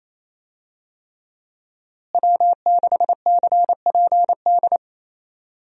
Morse code, heard as a keyed tone, about 28 wpm, 700 Hz. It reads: W6CPD
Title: USA Call Signs - 5 Characters - Repeated 3x - 28wpm